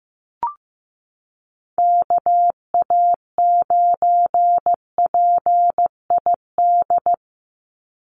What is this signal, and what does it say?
Morse code, heard as a keyed tone, about 15 wpm, 700 Hz. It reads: KA9PID
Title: USA Call Signs 15wpm